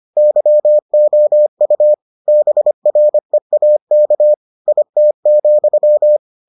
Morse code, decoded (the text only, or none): YOU BREAK IT,